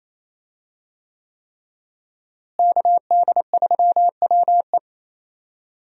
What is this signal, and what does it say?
Morse code, heard as a keyed tone, about 28 wpm, 700 Hz. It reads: KD3WE
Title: USA Call Signs - 5 Characters - Repeated 3x - 28wpm